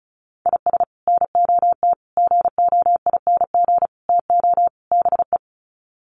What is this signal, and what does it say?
Morse code, heard as a keyed tone, about 34 wpm, 700 Hz. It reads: IS NOT GOING TO BE
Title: Sets of 5 Words Encore2 35wpm